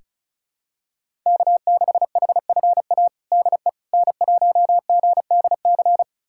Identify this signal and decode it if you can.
Morse code, heard as a keyed tone, about 34 wpm, 700 Hz. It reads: K6HFA DE N1GDC
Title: USA Callsign DE Callsign - 5 Characters - 35wpm